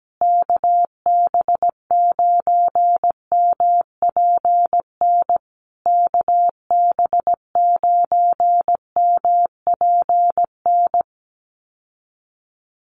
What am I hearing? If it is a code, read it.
Morse code, heard as a keyed tone, about 17 wpm, 700 Hz. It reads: KB9MPN KB9MPN
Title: USA Call Signs Repeated 3x 17wpm